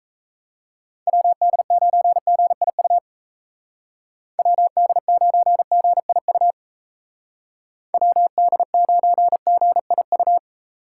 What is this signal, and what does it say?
Morse code, heard as a keyed tone, about 33 wpm, 700 Hz. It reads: WD9GIU WD9GIU WD9GIU
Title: Speed Racing - USA 2x3 Call Signs 28wpm